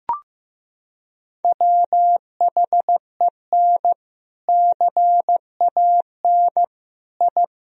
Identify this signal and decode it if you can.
Morse code, heard as a keyed tone, about 15 wpm, 700 Hz. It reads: WHEN CAN I
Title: Sentences from Top 1000 Words - Review Part 3 - 15wpm